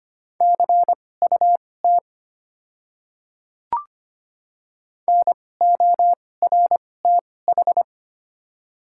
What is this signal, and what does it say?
Morse code, heard as a keyed tone, about 25 wpm, spaced as for 18 wpm, 690 Hz. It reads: CUT NORTH
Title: 2x Character Spacing - Top 500 Words - 25wpm